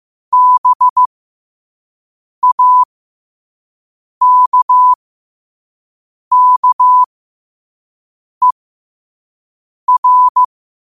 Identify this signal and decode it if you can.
Morse code, heard as a keyed tone, about 15 wpm, spaced as for 5 wpm, 1000 Hz. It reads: BAKKER
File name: qotdfiles / 5wpm